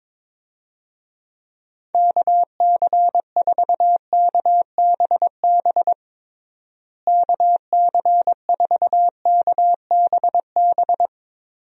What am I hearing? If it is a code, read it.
Morse code, heard as a keyed tone, about 22 wpm, 700 Hz. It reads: KC4KBB KC4KBB
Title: USA Call Signs - 6 Characters - Repeated 3x - 22wpm